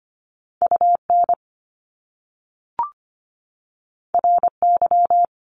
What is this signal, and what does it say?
Morse code, heard as a keyed tone, about 25 wpm, 700 Hz. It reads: UN RY
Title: Top Two Letter Combinations in Words 25wpm